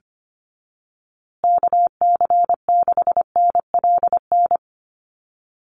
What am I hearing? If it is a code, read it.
Morse code, heard as a keyed tone, about 25 wpm, 700 Hz. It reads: KC6NLN
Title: USA Call Signs Repeated 3x 25wpm